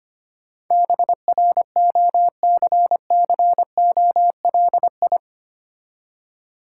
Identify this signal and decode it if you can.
Morse code, heard as a keyed tone, about 25 wpm, 700 Hz. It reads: BROCCOLI
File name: practice / 25wpm / Difficult Words with Hints - 25wpm